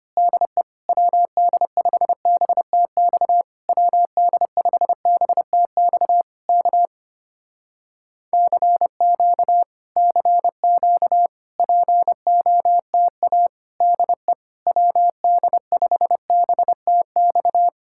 Morse code, heard as a keyed tone, about 25 wpm, 700 Hz. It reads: DE WD5BTX WD5BTX K CQ CQ POTA DE WD5BTX
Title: Speed-Racing Common QSO Phrases - 25wpm